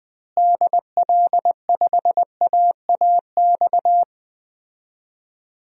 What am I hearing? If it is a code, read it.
Morse code, heard as a keyed tone, about 20 wpm, 700 Hz. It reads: DL5AAX
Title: DXCC-100 Call Signs - 6 Characters - Repeated 2x - 20wpm